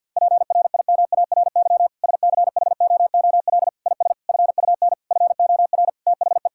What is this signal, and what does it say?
Morse code, heard as a keyed tone, about 50 wpm, 700 Hz. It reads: PRIMARY SCHOOL IS FUN FOR THE
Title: Sentences from Top 1000 Words - Review Part 5 - 50wpm